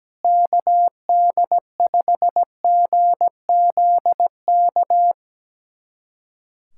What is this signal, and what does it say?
Morse code, heard as a keyed tone, about 17 wpm, 700 Hz. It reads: KD5GZK
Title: USA Callsign DE Callsign 17wpm